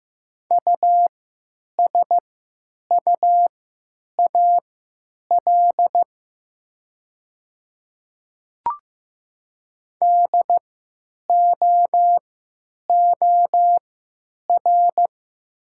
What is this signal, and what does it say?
Morse code, heard as a keyed tone, about 15 wpm, spaced as for 9 wpm, 700 Hz. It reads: USUAL DOOR
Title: 3x Character Spacing - Top 500 Words - 15wpm